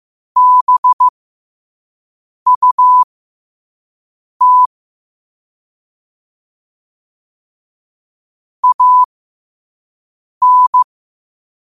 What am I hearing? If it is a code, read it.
Morse code, heard as a keyed tone, about 15 wpm, spaced as for 5 wpm, 1000 Hz. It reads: BUT AN
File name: qotdfiles / 5wpm